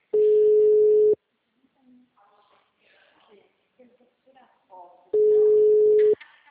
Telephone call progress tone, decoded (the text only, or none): ringback tone